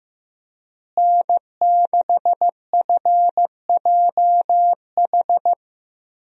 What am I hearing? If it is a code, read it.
Morse code, heard as a keyed tone, about 15 wpm, 700 Hz. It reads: N6FJH
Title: USA Call Signs - 5 Characters - Repeated 3x - 15wpm